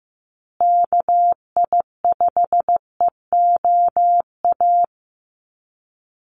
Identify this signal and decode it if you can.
Morse code, heard as a keyed tone, about 15 wpm, 700 Hz. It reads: KI5EOA
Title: USA Call Signs Repeated 3x 15wpm